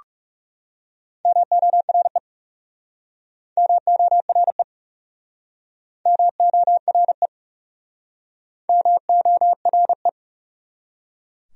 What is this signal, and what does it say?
Morse code, heard as a keyed tone, about 40 wpm, 700 Hz. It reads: MORE MORE MORE MORE